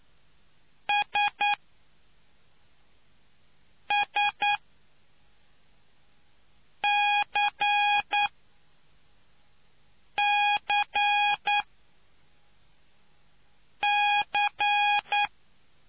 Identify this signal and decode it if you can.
Morse code, heard as a keyed tone, about 9 wpm, spaced as for 3 wpm, 820 Hz. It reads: SSCCC